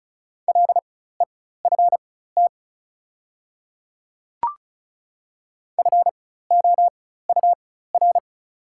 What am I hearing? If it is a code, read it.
Morse code, heard as a keyed tone, about 36 wpm, spaced as for 16 wpm, 690 Hz. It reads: LEFT FOUR
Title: 4x Character Spacing - Top 300 Words - 35wpm